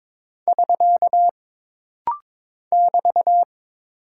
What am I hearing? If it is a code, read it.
Morse code, heard as a keyed tone, about 23 wpm, 700 Hz. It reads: <SK> <BT>
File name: practice / 22wpm / Single Punctuation-Prosign - Rapid-Fire - 22wpm